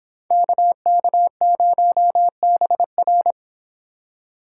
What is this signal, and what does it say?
Morse code, heard as a keyed tone, about 27 wpm, 700 Hz. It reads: KK0BR
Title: Speed Racing - USA 2x2 Call Signs 22wpm